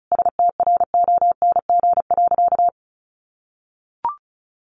Morse code, heard as a keyed tone, about 34 wpm, 700 Hz. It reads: STRONG.
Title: Sentences from Top 1400 Words 35wpm